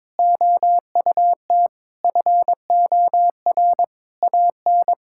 Morse code, heard as a keyed tone, about 22 wpm, 700 Hz. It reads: OUT FOR AN
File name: practice / 22wpm / Sets of 6 Words - 22wpm